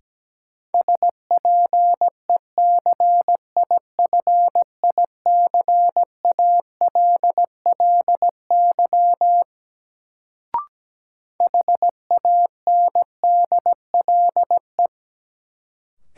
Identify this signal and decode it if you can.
Morse code, heard as a keyed tone, about 17 wpm, 700 Hz. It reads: SPECIFICALLY HANDLE
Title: Sentences from Top 1100 Words - New Word Drill Alt - 17wpm